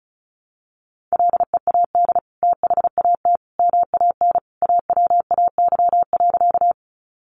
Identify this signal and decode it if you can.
Morse code, heard as a keyed tone, about 34 wpm, 700 Hz. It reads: LEAD THAT MAN AWAY.
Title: Sentences from Top 600 Words 35wpm